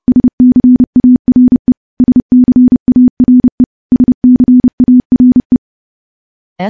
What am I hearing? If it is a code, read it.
Morse code, heard as a keyed tone, about 28 wpm, 250 Hz. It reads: SCARE SCARE SCARE
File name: morse / morse-training-30WPM